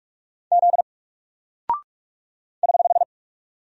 Morse code, heard as a keyed tone, about 44 wpm, 700 Hz. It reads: Z <HH>
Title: Single Character (Letter:Number:Punctuation:Prosign) 45wpm